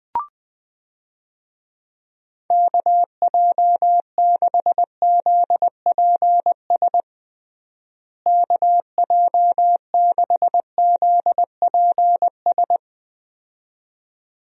Morse code, heard as a keyed tone, about 20 wpm, 700 Hz. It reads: KJ6ZPS KJ6ZPS
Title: USA Call Signs - 6 Characters - Repeated 2x - 20wpm